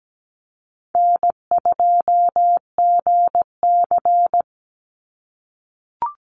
Morse code, heard as a keyed tone, about 17 wpm, 700 Hz. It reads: N2GC
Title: Easy Call Signs - 17wpm